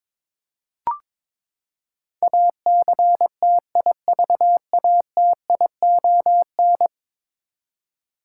Morse code, heard as a keyed tone, about 22 wpm, 700 Hz. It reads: ACTIVATION
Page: morse-code-ninja-practice-22wpm